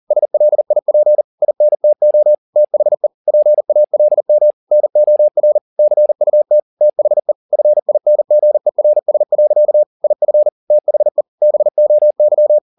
Morse code, heard as a keyed tone, about 40 wpm, 600 Hz. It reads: SLIP INTO THE PALM NOR CUT THE FINGERS. IF THE BOY
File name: the old man and the sea 2 40wpm